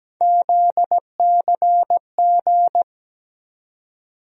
Morse code, heard as a keyed tone, about 17 wpm, 700 Hz. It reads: ZCG
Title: USA Call Sign Postfixes - 3 Letters 17wpm